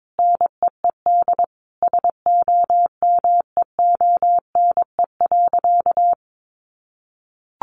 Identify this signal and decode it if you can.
Morse code, heard as a keyed tone, about 22 wpm, 700 Hz. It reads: NEED SOMEONE.
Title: Sentences from Top 1300 Words 22wpm